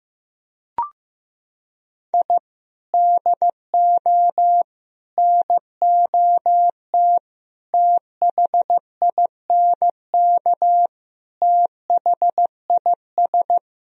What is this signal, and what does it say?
Morse code, heard as a keyed tone, about 15 wpm, 700 Hz. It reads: I DO NOT THINK THIS
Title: Sentences from Top 1000 Words - Review Part 4 - 15wpm